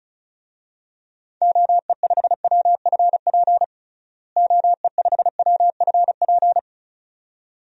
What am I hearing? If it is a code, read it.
Morse code, heard as a keyed tone, about 36 wpm, 700 Hz. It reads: OE5WFP OE5WFP
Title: DXCC-100 Call Signs - 6 Characters - Repeated 2x - 35wpm